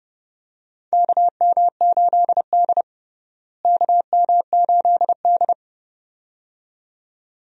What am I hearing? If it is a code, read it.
Morse code, heard as a keyed tone, about 30 wpm, 700 Hz. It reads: KM8D KM8D
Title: USA Call Signs - 4 Characters - Repeated 2x - 30wpm